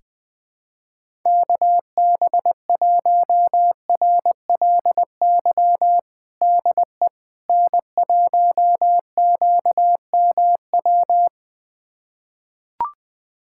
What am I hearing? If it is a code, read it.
Morse code, heard as a keyed tone, about 20 wpm, 700 Hz. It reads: KB1RLY DE N1QMW
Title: USA Callsign DE Callsign 20wpm